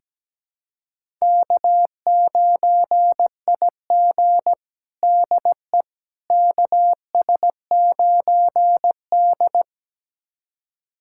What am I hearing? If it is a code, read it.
Morse code, heard as a keyed tone, about 17 wpm, 700 Hz. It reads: K9IG DE KS9D
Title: USA Callsign DE Callsign - 4 Characters - 17wpm